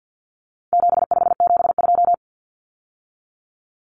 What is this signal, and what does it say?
Morse code, heard as a keyed tone, about 50 wpm, 700 Hz. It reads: 7572
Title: Numbers - 4-Digit 50wpm